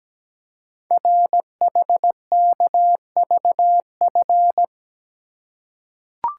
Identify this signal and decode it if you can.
Morse code, heard as a keyed tone, about 17 wpm, 700 Hz. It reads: RHKVF
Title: Code Groups - 5 Letters - 17wpm